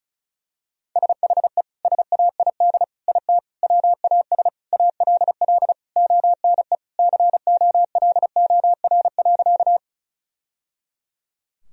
Morse code, heard as a keyed tone, about 36 wpm, 700 Hz. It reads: SHE SAID IT WAS ALL ONE COLOR.
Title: Sentences from Top 1000 Words - Review Part 2 - 35wpm